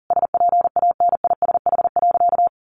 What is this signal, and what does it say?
Morse code, heard as a keyed tone, about 40 wpm, 700 Hz. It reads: SPANISH.
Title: Additional Easy Phrases 40wpm